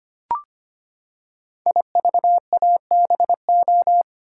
Morse code, heard as a keyed tone, about 25 wpm, 700 Hz. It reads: IVABO
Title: Code Groups - 5 Letters - 25wpm